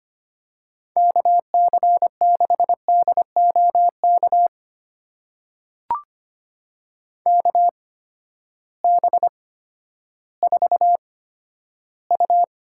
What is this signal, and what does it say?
Morse code, heard as a keyed tone, about 25 wpm, spaced as for 12 wpm, 700 Hz. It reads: KC6DOK KB4U
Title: USA Call Signs - 6 Characters - 8x Character Spacing - 25wpm